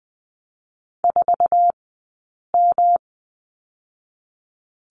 Morse code, heard as a keyed tone, about 20 wpm, spaced as for 8 wpm, 700 Hz. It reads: 4M
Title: 126 Lesson 21d - Number 6 - Two Characters 20wpm